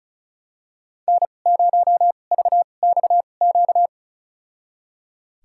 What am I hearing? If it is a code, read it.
Morse code, heard as a keyed tone, about 34 wpm, spaced as for 25 wpm, 700 Hz. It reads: N0VXQ
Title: USA Call Signs - 5 Characters - 2x Character Spacing - 35wpm